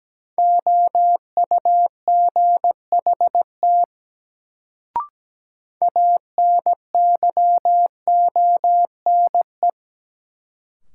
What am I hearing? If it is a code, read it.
Morse code, heard as a keyed tone, about 17 wpm, 700 Hz. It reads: OUGHT ANYONE
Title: Sets of 5 Words Encore - New Word Drill - 17wpm